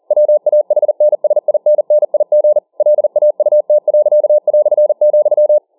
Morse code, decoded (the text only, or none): WAHNSINNIG LAUT.",